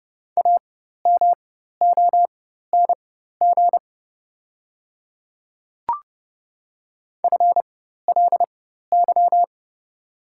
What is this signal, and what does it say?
Morse code, heard as a keyed tone, about 30 wpm, spaced as for 14 wpm, 690 Hz. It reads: AMONG FLY
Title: 4x Character Spacing - Top 500 Words - 30wpm